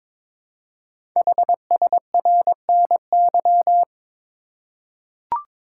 Morse code, heard as a keyed tone, about 22 wpm, 700 Hz. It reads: HSRNY
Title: Code Groups - 5 Letters - 22wpm